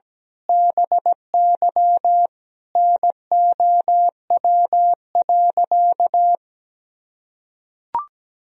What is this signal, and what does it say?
Morse code, heard as a keyed tone, about 17 wpm, 700 Hz. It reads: BY NOW.
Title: Sentences from Top 1000 Words - Review Part 3 - 17wpm